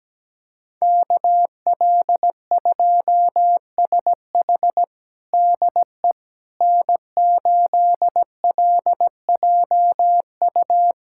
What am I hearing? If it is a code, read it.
Morse code, heard as a keyed tone, about 17 wpm, 700 Hz. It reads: KL2SH DE N8LJU
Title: USA Callsign DE Callsign 17wpm